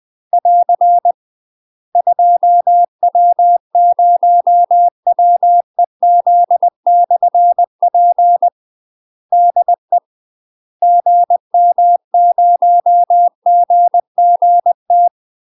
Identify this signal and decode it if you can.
Morse code, heard as a keyed tone, about 20 wpm, 700 Hz. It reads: <AR> 2W0WEZ/P DE GM0GGT
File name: UK MASTER QSOs TRAINING / 20WPM / UK MASTER QSOs 0071